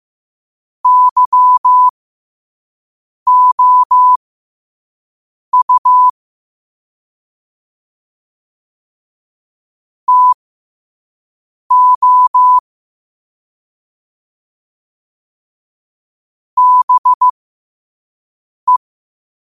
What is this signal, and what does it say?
Morse code, heard as a keyed tone, about 15 wpm, spaced as for 5 wpm, 1000 Hz. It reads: YOU TO BE
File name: qotdfiles / 5wpm